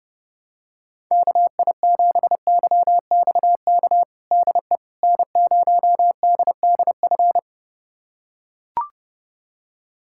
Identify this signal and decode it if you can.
Morse code, heard as a keyed tone, about 30 wpm, 700 Hz. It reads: KI7YXK DE N0DDF
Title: USA Callsign DE Callsign 30wpm